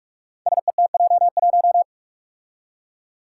Morse code, heard as a keyed tone, about 44 wpm, 700 Hz. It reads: SETJ1
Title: Five Challenging Letter-Number Combinations 45wpm